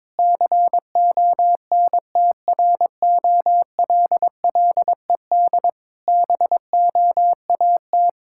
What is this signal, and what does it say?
Morse code, heard as a keyed tone, about 22 wpm, 690 Hz. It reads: CONTROLLED BOAT
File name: Difficult Words with Hints - 22wpm